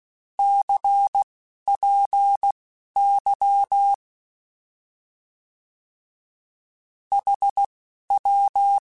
Morse code, heard as a keyed tone, about 16 wpm, spaced as for 11 wpm, 800 Hz. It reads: CPY HW
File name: Short Runs / Short Runs-mp3s / Words @10